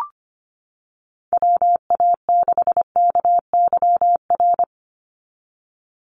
Morse code, heard as a keyed tone, about 25 wpm, 700 Hz. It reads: WA6KYR
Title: NA SOTA Chaser Callsigns 25wpm